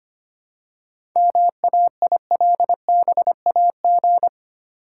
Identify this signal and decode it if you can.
Morse code, heard as a keyed tone, about 25 wpm, 690 Hz. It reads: MAILBAG
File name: practice / 25wpm / Easy Words with Hints - 25wpm